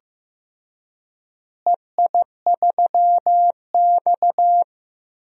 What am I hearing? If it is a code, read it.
Morse code, heard as a keyed tone, about 15 wpm, 700 Hz. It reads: EI3X
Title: DXCC-100 Call Signs - 4 Characters - Repeated 3x - 15wpm